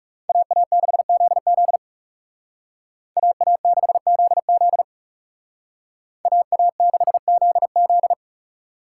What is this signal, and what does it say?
Morse code, heard as a keyed tone, about 38 wpm, 700 Hz. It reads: AA6ZZ AA6ZZ AA6ZZ